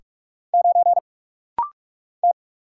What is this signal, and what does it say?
Morse code, heard as a keyed tone, about 44 wpm, 700 Hz. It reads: 9 T
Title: Single Letter-Number Alt - Rapid-Fire 45wpm